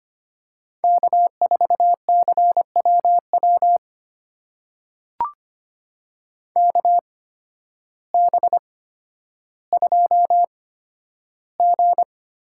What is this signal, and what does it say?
Morse code, heard as a keyed tone, about 25 wpm, spaced as for 12 wpm, 700 Hz. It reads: K4CWW KB2G
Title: USA Call Signs - 8x Character Spacing - 25wpm